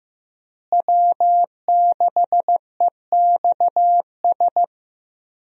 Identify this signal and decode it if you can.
Morse code, heard as a keyed tone, about 15 wpm, 700 Hz. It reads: W6EXS
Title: USA Call Signs - 5 Characters - Repeated 3x - 15wpm